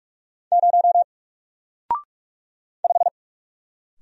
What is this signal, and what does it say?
Morse code, heard as a keyed tone, about 46 wpm, 700 Hz. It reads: 0 5
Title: Single Letter-Number Alt 45wpm